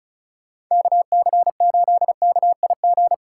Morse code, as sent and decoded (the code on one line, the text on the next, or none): -.- -.-. ---.. -.- .. --.
KC8KIG